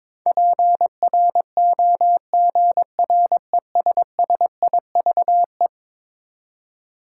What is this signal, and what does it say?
Morse code, heard as a keyed tone, about 22 wpm, 700 Hz. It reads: PROGRESSIVE
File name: practice / 22wpm / Eleven Letter Words - 22wpm